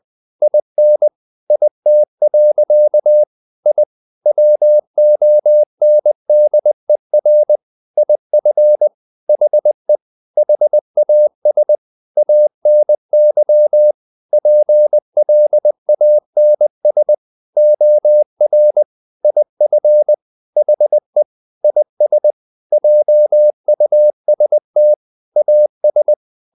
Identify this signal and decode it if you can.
Morse code, heard as a keyed tone, about 20 wpm, 600 Hz. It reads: IN IT. I WONDER IF HE HAS ANY PLANS OR IF HE IS JUST AS
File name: the old man and the sea / the old man and the sea part 1 20wpm